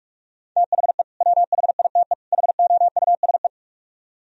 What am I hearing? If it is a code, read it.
Morse code, heard as a keyed tone, about 48 wpm, 700 Hz. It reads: THE WHITE HOUSE